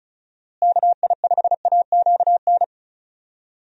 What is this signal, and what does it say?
Morse code, heard as a keyed tone, about 34 wpm, 700 Hz. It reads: KI5AQN
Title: USA Call Signs - 6 Characters - Repeated 2x - 35wpm